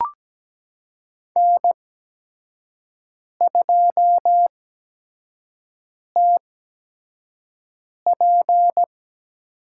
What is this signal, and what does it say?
Morse code, heard as a keyed tone, about 17 wpm, spaced as for 5 wpm, 700 Hz. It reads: N2TP
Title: USA Call Signs - 5 Characters - 8x Character Spacing - 17wpm